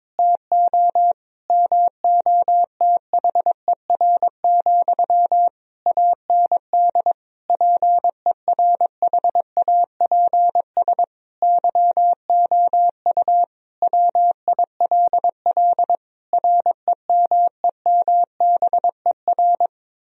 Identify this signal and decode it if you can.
Morse code, heard as a keyed tone, about 22 wpm, 700 Hz. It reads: TO MOTHER, AND PERHAPS YOU WILL REMEMBER
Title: Sentences from Top 1000 Words - Review Part 1 - 22wpm